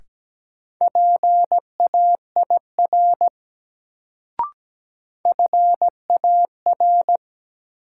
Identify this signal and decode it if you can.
Morse code, heard as a keyed tone, about 17 wpm, 700 Hz. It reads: PAIR FAR
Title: Sets of 3 Words - New Word Drill - 17wpm